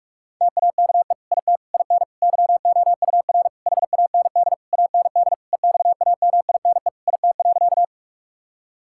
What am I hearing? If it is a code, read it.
Morse code, heard as a keyed tone, about 46 wpm, 700 Hz. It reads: TAKE IT IN YOUR HAND AND EXAMINE IT.
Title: Sentences from Top 1600 Words 45wpm